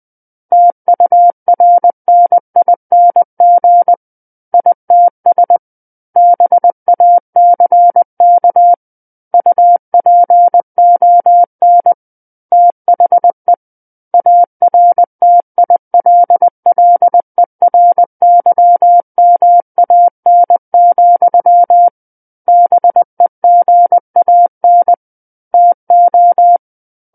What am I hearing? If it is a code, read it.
Morse code, heard as a keyed tone, about 20 wpm, 700 Hz. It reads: TURNING ITS BACK UPON THE ARTILLERYMAN, BEGAN TO
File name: 20 WPM